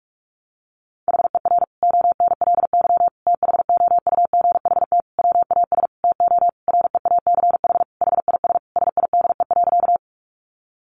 Morse code, heard as a keyed tone, about 44 wpm, 700 Hz. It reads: HER ONLY THOUGHT WAS TO REACH HIS SIDE.
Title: Sentences from Top 400 Words 45wpm